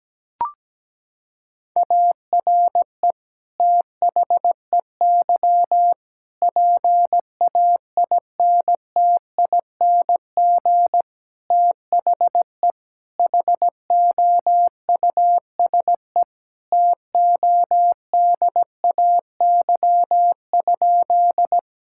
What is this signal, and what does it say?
Morse code, heard as a keyed tone, about 17 wpm, 700 Hz. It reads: ARE THEY PAINTING THE HOUSE TODAY?
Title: Sentences from Top 1000 Words - Review Part 6 - 17wpm